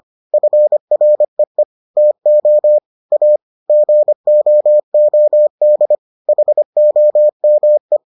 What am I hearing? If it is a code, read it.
Morse code, heard as a keyed tone, about 25 wpm, 600 Hz. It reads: FREE TO A GOOD HOME